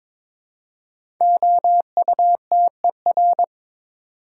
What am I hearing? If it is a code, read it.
Morse code, heard as a keyed tone, about 22 wpm, 700 Hz. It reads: OUTER
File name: 5 Letter Words - First 3 Letters Given - 22wpm